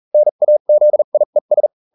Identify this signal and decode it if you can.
Morse code, heard as a keyed tone, about 40 wpm, spaced as for 34 wpm, 600 Hz. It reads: NAZIES